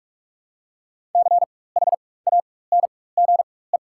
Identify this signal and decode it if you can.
Morse code, heard as a keyed tone, about 44 wpm, spaced as for 21 wpm, 700 Hz. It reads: CHANGE
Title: 4x Character Spacing - Top 200 Words - 45wpm